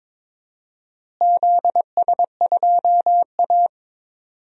Morse code, heard as a keyed tone, about 21 wpm, 700 Hz. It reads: ZS2A
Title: DXCC-100 Call Signs - 4 Characters - Repeated 2x - 22wpm